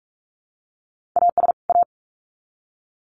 Morse code, heard as a keyed tone, about 44 wpm, 700 Hz. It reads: AS A
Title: Sets of 2 Words 45wpm